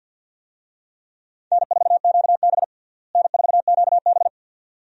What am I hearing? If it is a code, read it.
Morse code, heard as a keyed tone, about 52 wpm, 700 Hz. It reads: N4QB N4QB
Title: USA Call Signs - 4 Characters - Repeated 2x - 50wpm